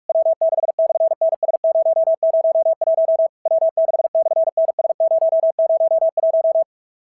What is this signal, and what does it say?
Morse code, heard as a keyed tone, about 44 wpm, 660 Hz. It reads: W6/NS001 W6/NS001
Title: SOTA QSO Practice 45wpm